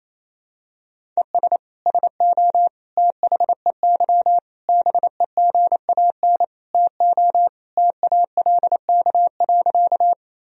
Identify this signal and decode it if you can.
Morse code, heard as a keyed tone, about 28 wpm, 700 Hz. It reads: ES SO THEY BEGAN TO TALK.
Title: Sentences from Top 400 Words and Abbreviations 28wpm